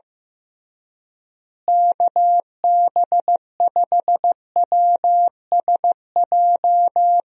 Morse code, heard as a keyed tone, about 16 wpm, 700 Hz. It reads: KB5WSJ
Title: USA Call Signs - 6 Characters - Repeated 3x - 15wpm